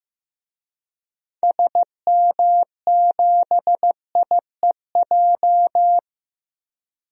Morse code, heard as a keyed tone, about 15 wpm, 700 Hz. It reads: SM7IEJ
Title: DXCC-100 Call Signs - 6 Characters - 15wpm